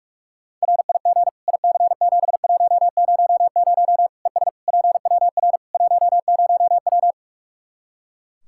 Morse code, heard as a keyed tone, about 46 wpm, 700 Hz. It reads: RIG IC7100 ES PWR 10W
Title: K1USN SST QSO Practice 45wpm